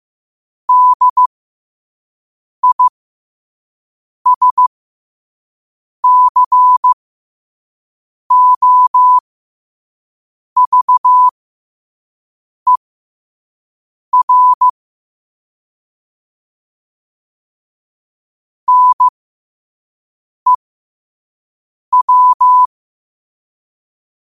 Morse code, heard as a keyed tone, about 15 wpm, spaced as for 5 wpm, 1000 Hz. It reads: DISCOVER NEW
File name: qotdfiles / 5wpm